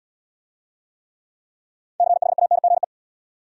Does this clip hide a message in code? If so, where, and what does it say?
Yes: Morse code from 2.0 to 2.8 s, heard as a keyed tone, about 80 wpm, 700 Hz. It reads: CHANGE